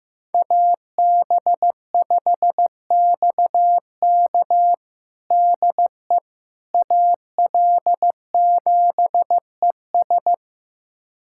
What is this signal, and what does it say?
Morse code, heard as a keyed tone, about 15 wpm, 700 Hz. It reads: AB5XK DE AL7ES
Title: USA Callsign DE Callsign - 5 Characters - 15wpm